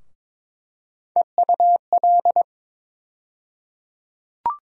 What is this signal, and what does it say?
Morse code, heard as a keyed tone, about 22 wpm, 700 Hz. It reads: EUL
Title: USA Call Sign Postfixes - 3 Letters 22wpm